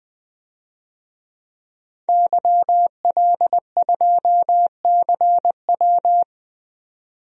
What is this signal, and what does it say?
Morse code, heard as a keyed tone, about 21 wpm, 700 Hz. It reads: YL2CW
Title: DXCC-100 Call Signs - 5 Characters - 20wpm